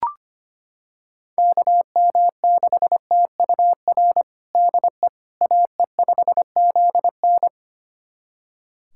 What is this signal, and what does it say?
Morse code, heard as a keyed tone, about 25 wpm, 700 Hz. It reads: KM6TUR DE AE5ZN
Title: USA Callsign DE Callsign 25wpm